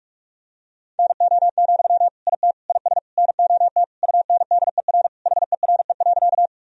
Morse code, heard as a keyed tone, about 46 wpm, 700 Hz. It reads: NO, IT IS NOT UNDER HERE.
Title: Sentences from Top 1000 Words - Review Part 1 - 45wpm